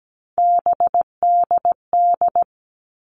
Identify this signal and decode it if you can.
Morse code, heard as a keyed tone, about 17 wpm, 700 Hz. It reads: BDD
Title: All Cut Numbers - 3-Digit 17wpm